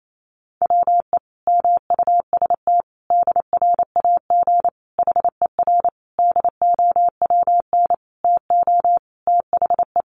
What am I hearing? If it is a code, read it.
Morse code, heard as a keyed tone, about 28 wpm, 700 Hz. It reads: WE MUST DRAG HER DOWN TO THE
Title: Sentences from Top 1500 Words 28wpm